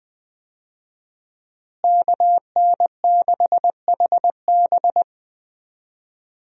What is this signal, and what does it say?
Morse code, heard as a keyed tone, about 20 wpm, 700 Hz. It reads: KN6HB
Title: USA Call Signs - 5 Characters - Repeated 3x - 20wpm